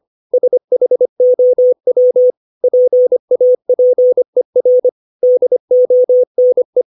Morse code, heard as a keyed tone, about 25 wpm, 500 Hz. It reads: SHOW PAPER DONE
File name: 632 words / 632 Words 25WPM